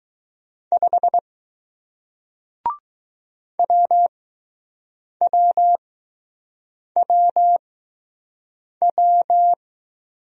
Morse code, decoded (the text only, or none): 5 W W W W